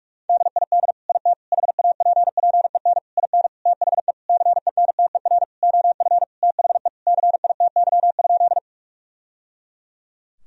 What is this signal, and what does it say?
Morse code, heard as a keyed tone, about 41 wpm, 700 Hz. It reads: DID IT HAPPEN IN THE CENTER OF THE CITY?
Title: Sentences from Top 1000 Words - Review Part 2 - 45wpm